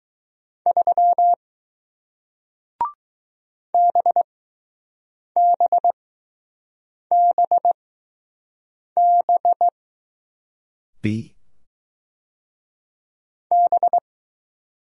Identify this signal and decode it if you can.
Morse code, heard as a keyed tone, about 20 wpm, spaced as for 6 wpm, 700 Hz. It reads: 3 BBBB B